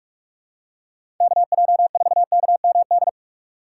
Morse code, heard as a keyed tone, about 42 wpm, 700 Hz. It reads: KJ4KMD
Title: USA Call Signs - 2x Character Spacing - 45wpm